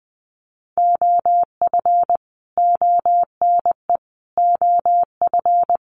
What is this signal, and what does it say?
Morse code, heard as a keyed tone, about 20 wpm, 700 Hz. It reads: OF ONE OF
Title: Sets of 4 Words 20wpm